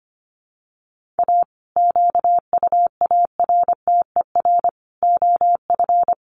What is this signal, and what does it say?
Morse code, heard as a keyed tone, about 25 wpm, 700 Hz. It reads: A QUARTER OF
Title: Sets of 5 Words Encore2 25wpm